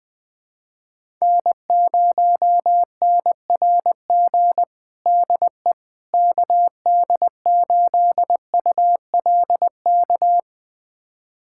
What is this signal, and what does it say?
Morse code, heard as a keyed tone, about 20 wpm, 700 Hz. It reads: N0NRG DE KD8ULK
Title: USA Callsign DE Callsign 20wpm